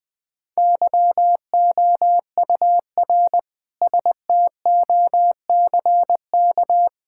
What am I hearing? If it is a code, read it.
Morse code, heard as a keyed tone, about 20 wpm, 700 Hz. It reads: YOUR STOCK
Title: Sentences from Top 1000 Words - Review Part 5 - 20wpm